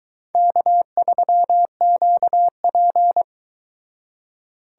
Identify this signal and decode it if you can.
Morse code, heard as a keyed tone, about 22 wpm, 700 Hz. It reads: K3QP